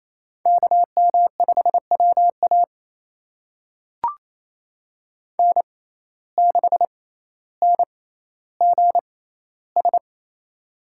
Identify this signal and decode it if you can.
Morse code, heard as a keyed tone, about 28 wpm, spaced as for 14 wpm, 700 Hz. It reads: KM5WA N6NGS
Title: USA Call Signs - 5 Characters - 6x Character Spacing - 28wpm